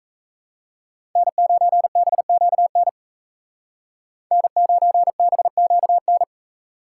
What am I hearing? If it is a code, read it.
Morse code, heard as a keyed tone, about 36 wpm, 700 Hz. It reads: N9BQN N9BQN